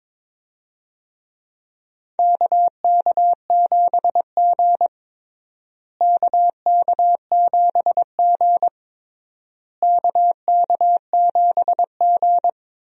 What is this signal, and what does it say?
Morse code, heard as a keyed tone, about 22 wpm, 700 Hz. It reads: KK7G KK7G KK7G
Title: USA Call Signs - 4 Characters - Repeated 3x - 22wpm